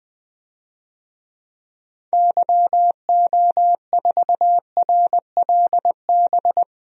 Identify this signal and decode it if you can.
Morse code, heard as a keyed tone, about 21 wpm, 700 Hz. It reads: YO4RLB
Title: DXCC-100 Call Signs - 6 Characters - Repeated 2x - 20wpm